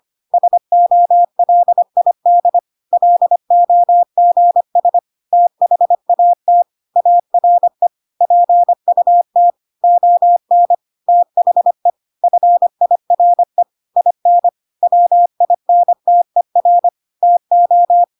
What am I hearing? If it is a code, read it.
Morse code, heard as a keyed tone, about 25 wpm, 700 Hz. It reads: SOLID LOGS THAT ARE PUT ON THE FIRE IN WINTER TO
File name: Pinocchio / Pinocchio ch1 25 25 ik1hge